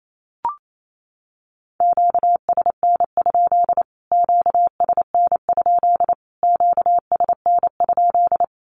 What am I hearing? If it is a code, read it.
Morse code, heard as a keyed tone, about 28 wpm, 700 Hz. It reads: QSN? QSN? QSN?
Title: Q-Codes Repeated 3x 28wpm